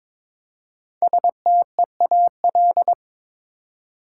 Morse code, heard as a keyed tone, about 22 wpm, 700 Hz. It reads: STEAL
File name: 5 Letter Words - First 3 Letters Given - 22wpm